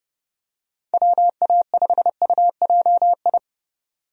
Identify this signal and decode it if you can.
Morse code, heard as a keyed tone, about 30 wpm, 700 Hz. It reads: WA5UJI
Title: USA Call Signs - 6 Characters - 30wpm